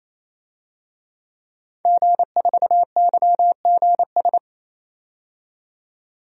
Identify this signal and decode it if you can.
Morse code, heard as a keyed tone, about 28 wpm, 700 Hz. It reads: G4YGS
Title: DXCC-100 Call Signs - 5 Characters - 28wpm